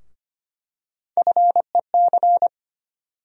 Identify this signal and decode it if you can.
Morse code, heard as a keyed tone, about 25 wpm, 700 Hz. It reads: FEC
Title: USA Call Sign Postfixes - 3 Letters 25wpm